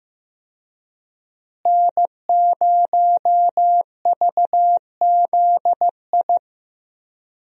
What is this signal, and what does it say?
Morse code, heard as a keyed tone, about 15 wpm, 700 Hz. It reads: N0VZI
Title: USA Call Signs - 5 Characters - Repeated 2x - 15wpm